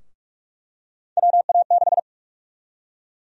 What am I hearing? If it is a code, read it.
Morse code, heard as a keyed tone, about 42 wpm, 700 Hz. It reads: WA6
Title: USA Call Sign Prefixes - Repeated 2x - 45wpm